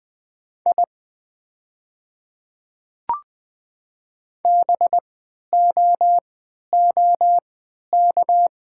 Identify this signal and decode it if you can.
Morse code, heard as a keyed tone, about 20 wpm, spaced as for 11 wpm, 700 Hz. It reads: I BOOK
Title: 3x Character Spacing - Top 300 Words - 20wpm